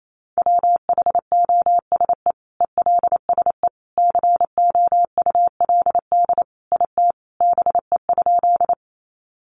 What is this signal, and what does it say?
Morse code, heard as a keyed tone, about 28 wpm, 700 Hz. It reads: WHOSE ELSE COULD IT BE?
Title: Sentences from Top 1200 Words 28wpm